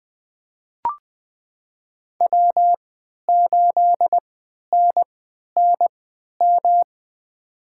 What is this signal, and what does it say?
Morse code, heard as a keyed tone, about 21 wpm, spaced as for 12 wpm, 700 Hz. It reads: W8NNM
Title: USA Call Signs - 5 Characters - 3x Character Spacing - 20wpm